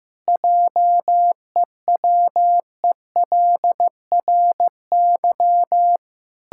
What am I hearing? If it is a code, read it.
Morse code, heard as a keyed tone, about 15 wpm, 700 Hz. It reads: JEWELRY
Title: Pangram Challenge v2 - 15wpm